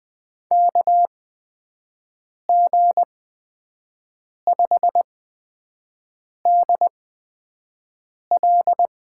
Morse code, heard as a keyed tone, about 21 wpm, spaced as for 5 wpm, 700 Hz. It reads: KG5DL
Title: USA Call Signs - 6 Characters - 8x Character Spacing - 20wpm